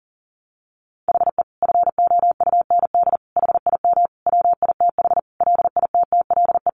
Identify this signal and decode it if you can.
Morse code, heard as a keyed tone, about 40 wpm, 700 Hz. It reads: HE FOUND HIM WITH LITTLE
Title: Sentences from Top 1500 Words 40wpm